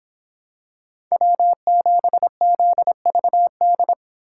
Morse code, heard as a keyed tone, about 26 wpm, 700 Hz. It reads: W7ZVD